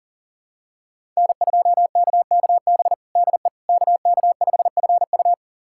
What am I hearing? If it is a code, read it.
Morse code, heard as a keyed tone, about 40 wpm, 700 Hz. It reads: N2KKB DE KK5FU
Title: USA Callsign DE Callsign 40wpm